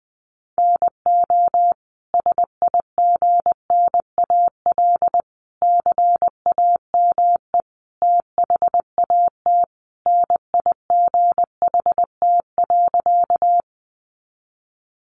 Morse code, heard as a keyed tone, about 19 wpm, 700 Hz. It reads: NO SIGNAL CAME THAT NIGHT.
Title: Sentences from Top 1400 Words 20wpm